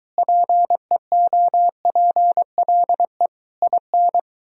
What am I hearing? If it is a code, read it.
Morse code, heard as a keyed tone, about 23 wpm, 700 Hz. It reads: PEOPLE IN